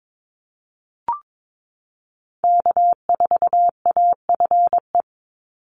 Morse code, heard as a keyed tone, about 22 wpm, 700 Hz. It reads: K4AFE
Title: Call Signs Repeated 2x - 22wpm